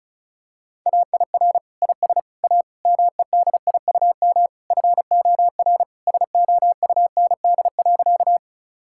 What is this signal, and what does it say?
Morse code, heard as a keyed tone, about 34 wpm, 700 Hz. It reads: AIR IS A MEDIUM FOR SOUND.
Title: Sentences from Top 1000 Words - Review Part 6 - 35wpm